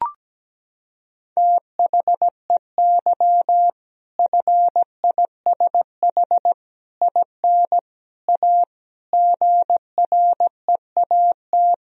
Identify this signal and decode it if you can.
Morse code, heard as a keyed tone, about 17 wpm, 700 Hz. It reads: THEY FISH IN A GREAT
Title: Sentences from Top 1000 Words - Review Part 1 - 17wpm